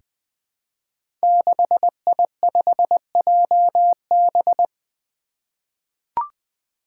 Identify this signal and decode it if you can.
Morse code, heard as a keyed tone, about 20 wpm, 700 Hz. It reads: 6I5JB
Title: Five Challenging Letter-Number Combinations 20wpm